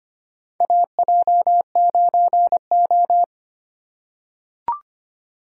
Morse code, heard as a keyed tone, about 25 wpm, 700 Hz. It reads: AJ9O
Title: USA Call Signs - 4 Characters - 3x Character Spacing - 25wpm